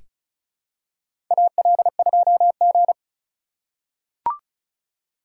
Morse code, heard as a keyed tone, about 34 wpm, 700 Hz. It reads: AL2G
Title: USA Call Signs - 4 Characters - 4x Character Spacing - 35wpm